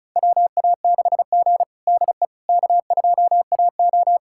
Morse code, heard as a keyed tone, about 36 wpm, 700 Hz. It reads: WA6G DE K2AO
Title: USA Callsign DE Callsign - 4 Characters - 35wpm